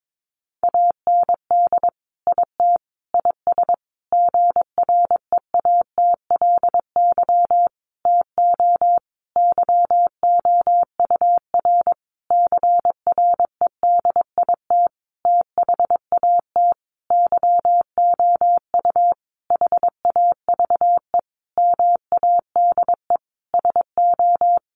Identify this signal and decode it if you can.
Morse code, heard as a keyed tone, about 22 wpm, 700 Hz. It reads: AND IT IS GREATLY TO YOUR CREDIT THAT YOU HAVE MADE SO
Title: Sentences from Top 1400 Words 22wpm